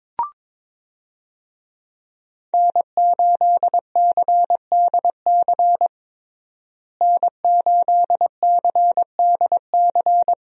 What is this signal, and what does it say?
Morse code, heard as a keyed tone, about 22 wpm, 700 Hz. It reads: N8CDC N8CDC
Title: USA Call Signs - 5 Characters - Repeated 2x - 22wpm